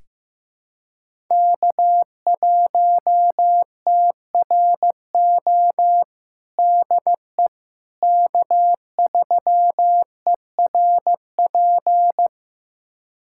Morse code, heard as a keyed tone, about 15 wpm, 700 Hz. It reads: K1TRO DE K3ERP
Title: USA Callsign DE Callsign 15wpm